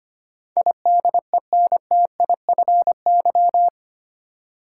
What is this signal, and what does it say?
Morse code, heard as a keyed tone, about 25 wpm, 700 Hz. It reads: IDENTIFY
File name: 8 Letter Words - First 3 Letters Given - 25wpm